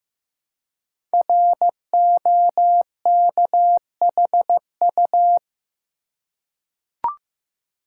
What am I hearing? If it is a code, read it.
Morse code, heard as a keyed tone, about 15 wpm, 700 Hz. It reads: ROKHU
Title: Code Groups - 5 Letters - 15wpm